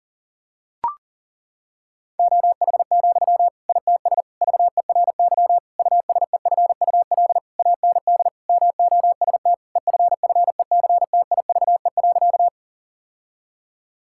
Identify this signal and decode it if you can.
Morse code, heard as a keyed tone, about 40 wpm, 700 Hz. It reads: OH, ITS VERY USEFUL AND MOST EFFECTIVE.
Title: Sentences from Top 1000 Words - Review Part 6 - 40wpm